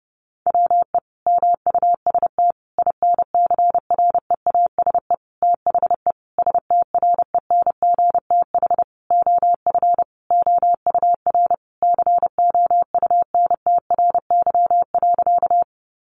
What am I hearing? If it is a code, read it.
Morse code, heard as a keyed tone, about 30 wpm, 700 Hz. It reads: WE MUST INCREASE THE STRENGTH OF OUR COUNTRY.
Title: Sentences from Top 1300 Words 30wpm